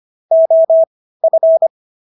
Morse code, heard as a keyed tone, about 25 wpm, spaced as for 15 wpm, 650 Hz. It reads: OF
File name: SS110 15